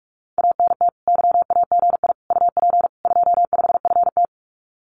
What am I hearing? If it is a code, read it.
Morse code, heard as a keyed tone, about 46 wpm, 700 Hz. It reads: ANT YAGI UP 25FT
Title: Common QSO Phrases Encore 45wpm